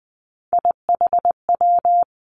Morse code, heard as a keyed tone, about 19 wpm, 700 Hz. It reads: IHW
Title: Three Challenging Letter-Number Combinations 20wpm